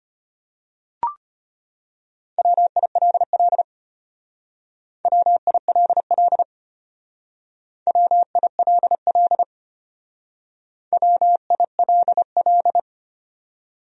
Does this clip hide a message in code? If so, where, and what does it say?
Yes: Morse code from 2.4 to 12.8 s, heard as a keyed tone, about 32 wpm, 700 Hz. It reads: WILL WILL WILL WILL